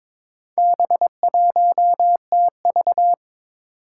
Morse code, heard as a keyed tone, about 23 wpm, 700 Hz. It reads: B1TV
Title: Four Challenging Letter-Number Combinations 22wpm